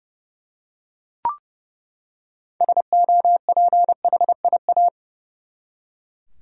Morse code, heard as a keyed tone, about 30 wpm, 700 Hz. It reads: SOPHIA
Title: USA Common Names - 30wpm